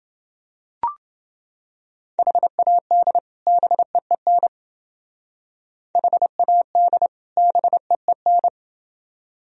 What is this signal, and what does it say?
Morse code, heard as a keyed tone, about 28 wpm, 700 Hz. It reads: HAD BEEN HAD BEEN